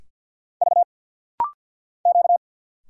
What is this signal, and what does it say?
Morse code, heard as a keyed tone, about 47 wpm, 700 Hz. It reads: V Q
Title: Single Letters - Rapid-Fire 50wpm